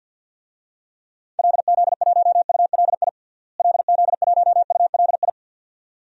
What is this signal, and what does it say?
Morse code, heard as a keyed tone, about 49 wpm, 700 Hz. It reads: LZ1ULI LZ1ULI
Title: DXCC-100 Call Signs - 6 Characters - Repeated 2x - 50wpm